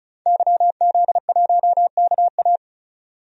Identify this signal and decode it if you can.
Morse code, heard as a keyed tone, about 34 wpm, 700 Hz. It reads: YZ1KA
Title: DXCC-100 Call Signs - 5 Characters - 35wpm